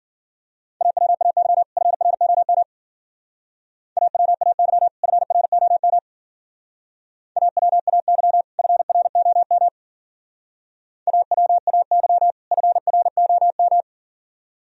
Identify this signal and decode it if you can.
Morse code, heard as a keyed tone, about 50 wpm, 690 Hz. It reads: AWAY FROM AWAY FROM AWAY FROM AWAY FROM